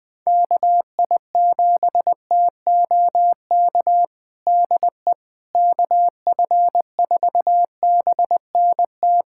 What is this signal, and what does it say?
Morse code, heard as a keyed tone, about 20 wpm, 700 Hz. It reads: KI7TOK DE KF4BNT
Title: USA Callsign DE Callsign - 6 Characters - 20wpm